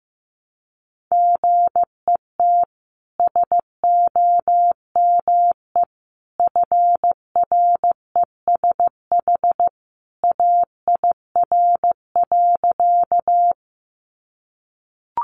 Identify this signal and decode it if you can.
Morse code, heard as a keyed tone, about 15 wpm, 700 Hz. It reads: GET SOME FRESH AIR.
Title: Sentences from Top 1200 Words 15wpm